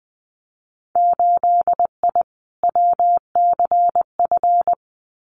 Morse code, heard as a keyed tone, about 21 wpm, 700 Hz. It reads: 8I WCF
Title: Winter Field Day Exchanges Encore 20wpm